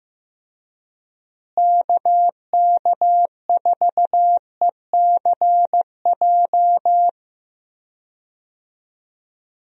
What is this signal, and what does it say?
Morse code, heard as a keyed tone, about 15 wpm, 700 Hz. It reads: KK4ECJ
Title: USA Call Signs - 6 Characters - Repeated 3x - 15wpm